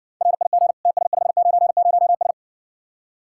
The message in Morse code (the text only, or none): RIG TS590S